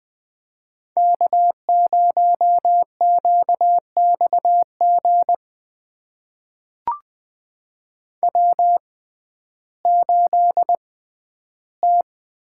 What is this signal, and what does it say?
Morse code, heard as a keyed tone, about 20 wpm, spaced as for 12 wpm, 700 Hz. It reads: K0QXG W8T
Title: USA Call Signs - 5 Characters - 6x Character Spacing - 20wpm